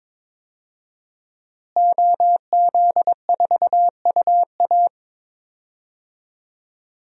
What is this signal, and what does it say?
Morse code, heard as a keyed tone, about 22 wpm, 700 Hz. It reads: OZ4UA
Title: DXCC-100 Call Signs - 5 Characters - Repeated 2x - 22wpm